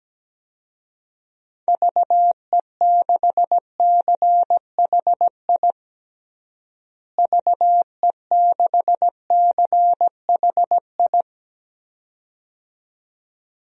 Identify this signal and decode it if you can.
Morse code, heard as a keyed tone, about 17 wpm, 700 Hz. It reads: VE6CHI VE6CHI
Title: DXCC-100 Call Signs - 6 Characters - Repeated 2x - 17wpm